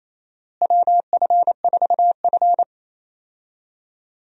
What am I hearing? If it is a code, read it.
Morse code, heard as a keyed tone, about 28 wpm, 700 Hz. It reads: WF4F
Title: USA Call Signs - 4 Characters - Repeated 3x - 28wpm